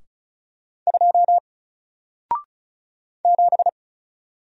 Morse code, heard as a keyed tone, about 36 wpm, 700 Hz. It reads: 2 7
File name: Single-Letter-Number-35wpm